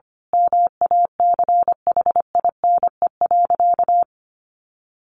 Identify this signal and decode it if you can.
Morse code, heard as a keyed tone, about 25 wpm, 700 Hz. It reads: MACHINE.
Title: A Sound Of Thunder 25wpm